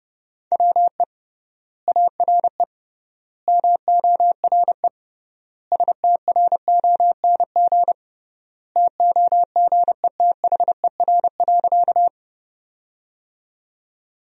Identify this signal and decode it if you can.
Morse code, heard as a keyed tone, about 32 wpm, 700 Hz. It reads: WE ARE MORE STRONG TOGETHER.
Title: 3x Word-Spacing - Sentences from Top 500 Words - 30wpm